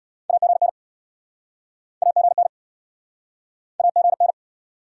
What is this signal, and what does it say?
Morse code, heard as a keyed tone, about 72 wpm, 700 Hz. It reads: AGN AGN AGN